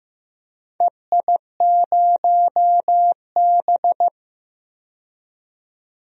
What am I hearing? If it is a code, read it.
Morse code, heard as a keyed tone, about 15 wpm, 700 Hz. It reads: EI0B
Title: DXCC-100 Call Signs - 4 Characters - 15wpm